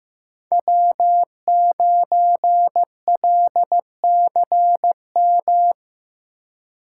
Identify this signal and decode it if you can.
Morse code, heard as a keyed tone, about 15 wpm, 700 Hz. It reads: W9LCM
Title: USA Call Signs - 5 Characters - 3x Character Spacing - 15wpm